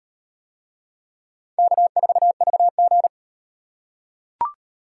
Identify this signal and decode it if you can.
Morse code, heard as a keyed tone, about 38 wpm, 700 Hz. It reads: K4VG